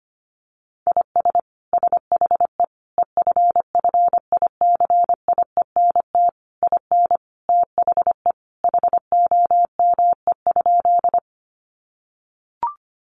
Morse code, heard as a keyed tone, about 25 wpm, 700 Hz. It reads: IS SHE EFFICIENT IN THE HOME?
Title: Sentences from Top 1300 Words 25wpm